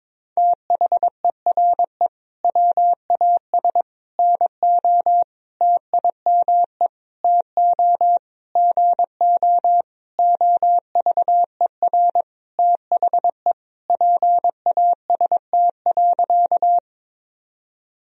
Morse code, heard as a keyed tone, about 22 wpm, 700 Hz. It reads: THERE WAS NO TIME TO GO OVER THE PAST.
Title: Sentences from Top 1000 Words - Review Part 3 - 22wpm